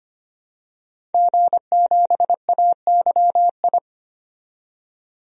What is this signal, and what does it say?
Morse code, heard as a keyed tone, about 25 wpm, 690 Hz. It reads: G7AYI
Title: DXCC-100 Call Signs - 5 Characters - 25wpm